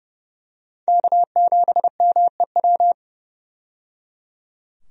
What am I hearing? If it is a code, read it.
Morse code, heard as a keyed tone, about 30 wpm, 700 Hz. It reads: K7MEW
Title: USA Call Signs - 5 Characters - 30wpm